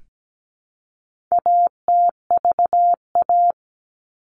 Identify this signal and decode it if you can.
Morse code, heard as a keyed tone, about 17 wpm, 700 Hz. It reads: ATVA
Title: All Cut Numbers - 4-Digit 17wpm